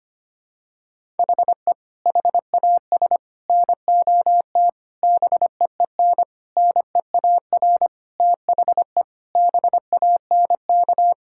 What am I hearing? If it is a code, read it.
Morse code, heard as a keyed tone, about 25 wpm, 700 Hz. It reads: HE HAS NOT BEEN NEAR THE BANK
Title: Sentences from Top 1000 Words - Review Part 6 - 25wpm